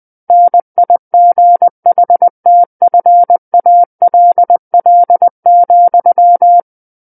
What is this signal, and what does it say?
Morse code, heard as a keyed tone, about 20 wpm, 700 Hz. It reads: NIGHTFALL,
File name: 20 WPM